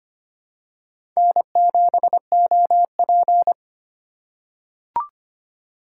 Morse code, heard as a keyed tone, about 25 wpm, 700 Hz. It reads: N7OP
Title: USA Call Signs - 4 Characters - 4x Character Spacing - 25wpm